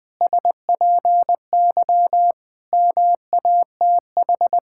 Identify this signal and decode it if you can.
Morse code, heard as a keyed tone, about 20 wpm, 700 Hz. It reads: SPY MATH
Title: Pangram Challenge v2 - 20wpm